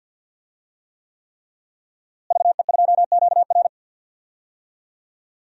Morse code, heard as a keyed tone, about 50 wpm, 700 Hz. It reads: VE2QR
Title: DXCC-100 Call Signs - 5 Characters - Repeated 3x - 50wpm